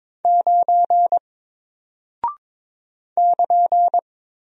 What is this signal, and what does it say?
Morse code, heard as a keyed tone, about 22 wpm, 700 Hz. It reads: 9 <KN>
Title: Single Character (Letter:Number:Punctuation:Prosign) Alt 22wpm